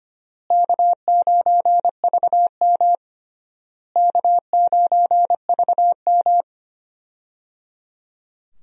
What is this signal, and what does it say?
Morse code, heard as a keyed tone, about 25 wpm, 700 Hz. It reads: K9VM K9VM
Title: USA Call Signs - 4 Characters - Repeated 3x - 25wpm